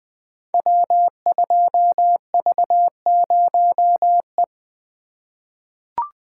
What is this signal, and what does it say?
Morse code, heard as a keyed tone, about 20 wpm, 700 Hz. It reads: W2V0E
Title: Code Groups - 5 Letter:Numbers - 20wpm